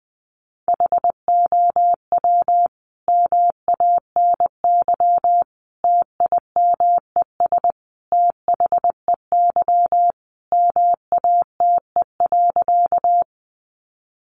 Morse code, heard as a keyed tone, about 20 wpm, 700 Hz. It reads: HOW MANY TIMES THEY MATE.
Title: A Sound Of Thunder 20wpm